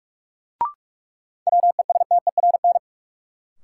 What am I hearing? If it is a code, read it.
Morse code, heard as a keyed tone, about 43 wpm, 700 Hz. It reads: WESTERN
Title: Sentences from Top 1000 Words - Review Part 7 - New Word Drill - 45wpm